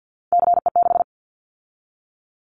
Morse code, heard as a keyed tone, about 50 wpm, 700 Hz. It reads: CE6
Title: Callsign Prefixes for Top 100 DXCC - 50wpm